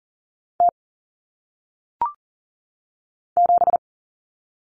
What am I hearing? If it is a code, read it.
Morse code, heard as a keyed tone, about 39 wpm, 700 Hz. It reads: T 7
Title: Common Cut Numbers - 1-Digit 40wpm